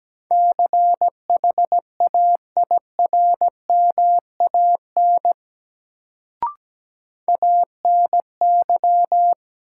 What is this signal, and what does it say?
Morse code, heard as a keyed tone, about 17 wpm, 700 Hz. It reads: CHAIRMAN ANY
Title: Sets of 3 Words - New Word Drill - 17wpm